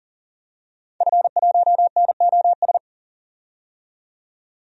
Morse code, heard as a keyed tone, about 40 wpm, 700 Hz. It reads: F1NOS
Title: DXCC-100 Call Signs - 5 Characters - 40wpm